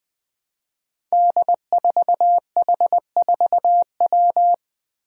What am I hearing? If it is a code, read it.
Morse code, heard as a keyed tone, about 20 wpm, 700 Hz. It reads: D4H4W
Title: Five Challenging Letter-Number Combinations 20wpm